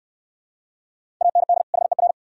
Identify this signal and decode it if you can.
Morse code, heard as a keyed tone, about 62 wpm, 690 Hz. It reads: AND HER